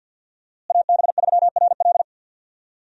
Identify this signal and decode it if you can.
Morse code, heard as a keyed tone, about 52 wpm, 700 Hz. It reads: AB3RL
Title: USA Call Signs - 5 Characters - Repeated 3x - 50wpm